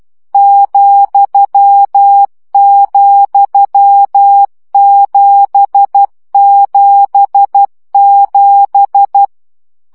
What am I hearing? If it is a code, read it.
Morse code, heard as a keyed tone, about 12 wpm, 800 Hz. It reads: ,,777
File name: CW gyakorlas z 7 vesszo 10 baud